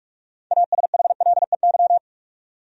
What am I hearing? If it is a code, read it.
Morse code, heard as a keyed tone, about 46 wpm, 700 Hz. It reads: ASHLEY
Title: USA Common Names - 45wpm